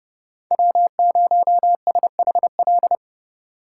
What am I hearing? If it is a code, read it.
Morse code, heard as a keyed tone, about 30 wpm, 700 Hz. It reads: W0SHL
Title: K1USN SST QSO Practice - 30wpm